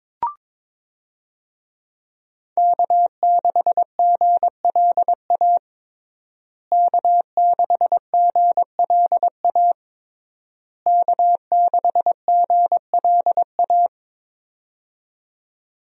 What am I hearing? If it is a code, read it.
Morse code, heard as a keyed tone, about 22 wpm, 700 Hz. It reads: K6GLA K6GLA K6GLA
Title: USA Call Signs - 5 Characters - Repeated 3x - 22wpm